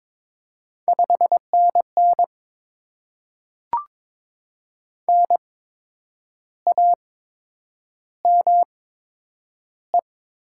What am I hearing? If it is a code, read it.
Morse code, heard as a keyed tone, about 23 wpm, spaced as for 9 wpm, 700 Hz. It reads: 5NN NAME
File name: Top 20 QSO Elements - 8x Character Spacing - 22wpm